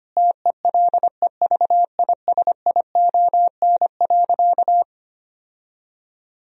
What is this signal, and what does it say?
Morse code, heard as a keyed tone, about 25 wpm, 700 Hz. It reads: TELEVISION.
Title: Sentences from Top 1000 Words - Review Part 4 - 25wpm